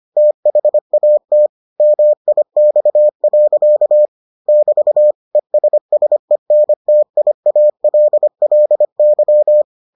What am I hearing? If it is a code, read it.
Morse code, heard as a keyed tone, about 25 wpm, 600 Hz. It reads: THAT MIX. <BT> ESSENTIALLY